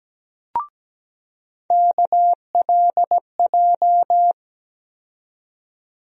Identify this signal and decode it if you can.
Morse code, heard as a keyed tone, about 17 wpm, 700 Hz. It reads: KLJ
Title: USA Call Sign Postfixes - 3 Letters 17wpm